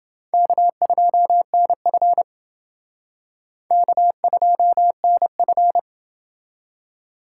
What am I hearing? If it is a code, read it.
Morse code, heard as a keyed tone, about 28 wpm, 700 Hz. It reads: K2NF K2NF